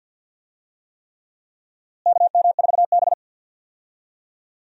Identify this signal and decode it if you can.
Morse code, heard as a keyed tone, about 52 wpm, 700 Hz. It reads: KM4B
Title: USA Call Signs - 4 Characters - Repeated 3x - 50wpm